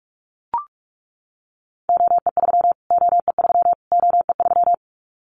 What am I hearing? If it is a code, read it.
Morse code, heard as a keyed tone, about 44 wpm, 700 Hz. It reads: OE3 OE3 OE3
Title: Callsign Prefixes for Top 25 DXCC - Repeated 3x - 45wpm